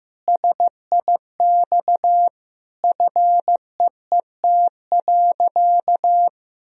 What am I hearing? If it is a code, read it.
Morse code, heard as a keyed tone, about 16 wpm, 700 Hz. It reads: SIX FEET.
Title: Sentences from Top 1000 Words - Review Part 2 - 15wpm